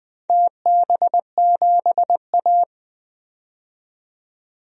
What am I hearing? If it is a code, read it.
Morse code, heard as a keyed tone, about 20 wpm, 700 Hz. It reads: TB7A
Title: DXCC-100 Call Signs - 4 Characters - 20wpm